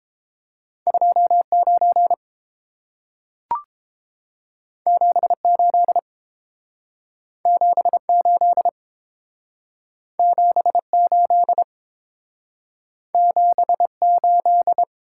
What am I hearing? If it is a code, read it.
Morse code, heard as a keyed tone, about 30 wpm, 700 Hz. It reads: 29 78 78 78 78